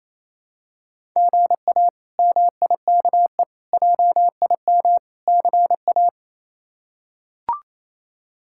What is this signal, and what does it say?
Morse code, heard as a keyed tone, about 28 wpm, 700 Hz. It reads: GA MIKE JIM CA
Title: K1USN SST QSO Practice - 28wpm